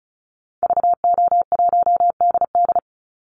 Morse code, heard as a keyed tone, about 34 wpm, 700 Hz. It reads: VO1DD
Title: Call Signs Repeated 3x - 35wpm